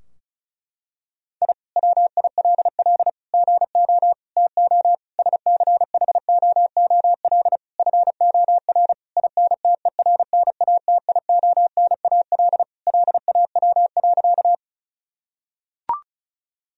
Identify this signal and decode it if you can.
Morse code, heard as a keyed tone, about 36 wpm, 700 Hz. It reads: I WILL GO TO SCHOOL FOR INTERNATIONAL LAW.
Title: Sentences from Top 1000 Words - Review Part 4 - 35wpm